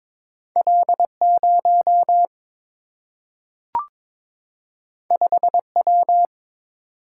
Morse code, heard as a keyed tone, about 22 wpm, 700 Hz. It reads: L 0 5 W
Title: Single-Letter Single-Number 22wpm